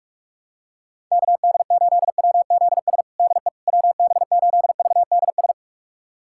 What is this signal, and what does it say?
Morse code, heard as a keyed tone, about 41 wpm, 700 Hz. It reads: KD8WZS DE WB8VDS
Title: USA Callsign DE Callsign - 6 Characters - 45wpm